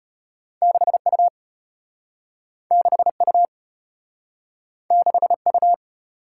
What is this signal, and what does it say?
Morse code, heard as a keyed tone, about 32 wpm, 700 Hz. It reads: 6U 6U 6U